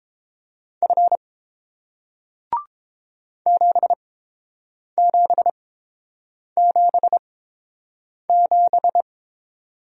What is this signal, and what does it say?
Morse code, heard as a keyed tone, about 29 wpm, spaced as for 7 wpm, 700 Hz. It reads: F 7777